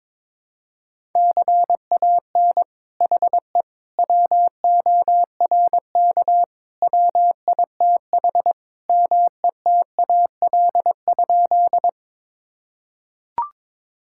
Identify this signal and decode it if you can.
Morse code, heard as a keyed tone, about 22 wpm, 700 Hz. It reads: CAN HE WORK WITH METAL?
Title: Sentences from Top 1000 Words - Review Part 6 - 22wpm